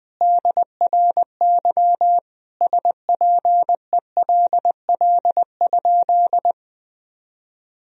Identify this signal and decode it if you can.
Morse code, heard as a keyed tone, about 21 wpm, 700 Hz. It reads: DRY SPELL?
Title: Sentences from Top 1000 Words - Review Part 3 - 20wpm